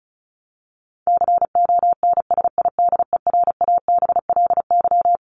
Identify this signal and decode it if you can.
Morse code, heard as a keyed tone, about 34 wpm, 700 Hz. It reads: CONSIDERABLY
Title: LY Suffix - Top 100 Words with Base Words 35wpm